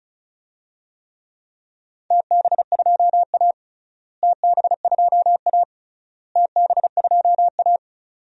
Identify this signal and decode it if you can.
Morse code, heard as a keyed tone, about 36 wpm, 700 Hz. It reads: TB2A TB2A TB2A
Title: DXCC-100 Call Signs - 4 Characters - Repeated 3x - 35wpm